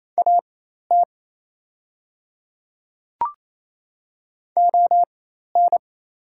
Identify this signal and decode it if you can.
Morse code, heard as a keyed tone, about 27 wpm, spaced as for 13 wpm, 700 Hz. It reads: AT ON
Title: 4x Character Spacing - Top 200 Words - 28wpm